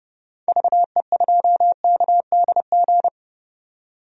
Morse code, heard as a keyed tone, about 30 wpm, 700 Hz. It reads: VE2KDG
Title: DXCC-100 Call Signs - 6 Characters - 30wpm